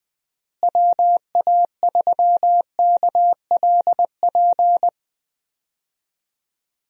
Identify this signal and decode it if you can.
Morse code, heard as a keyed tone, about 20 wpm, 700 Hz. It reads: WA3KLP
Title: USA Call Signs - 6 Characters - Repeated 2x - 20wpm